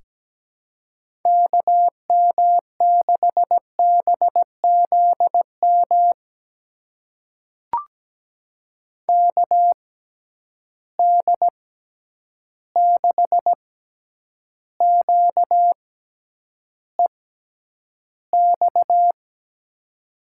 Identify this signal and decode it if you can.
Morse code, heard as a keyed tone, about 17 wpm, spaced as for 9 wpm, 700 Hz. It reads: KM6BZM KD6QEX
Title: USA Call Signs - 6 Characters - 6x Character Spacing - 17wpm